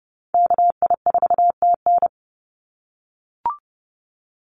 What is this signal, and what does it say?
Morse code, heard as a keyed tone, about 30 wpm, 700 Hz. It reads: KI4TN
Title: NA SOTA Activator Callsigns 30wpm